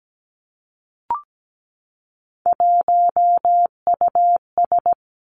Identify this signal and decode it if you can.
Morse code, heard as a keyed tone, about 17 wpm, 700 Hz. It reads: 1US
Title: Three Challenging Letter-Number Combinations 17wpm